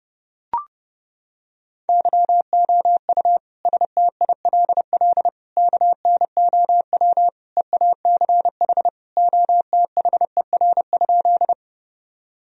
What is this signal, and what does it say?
Morse code, heard as a keyed tone, about 30 wpm, 700 Hz. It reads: YOU STILL KNOW EACH OTHER?
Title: Sentences from Top 1000 Words - Review Part 1 - 30wpm